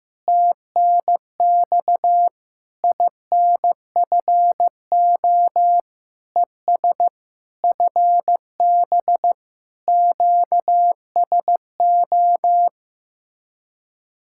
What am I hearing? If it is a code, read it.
Morse code, heard as a keyed tone, about 16 wpm, 700 Hz. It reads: TNX INFO ES FB QSO
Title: Speed-Racing Common QSO Phrases Encore - 15wpm